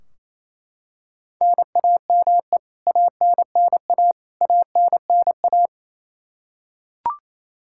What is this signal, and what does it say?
Morse code, heard as a keyed tone, about 28 wpm, 700 Hz. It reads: NAME ANNA ANNA
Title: K1USN SST QSO Practice 28wpm